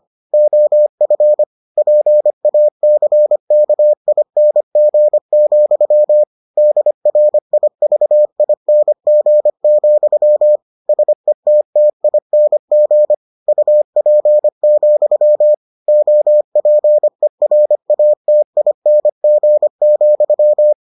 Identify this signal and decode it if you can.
Morse code, heard as a keyed tone, about 25 wpm, 600 Hz. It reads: OF PACKING, DRIVING, SETTING UP, OPERATING,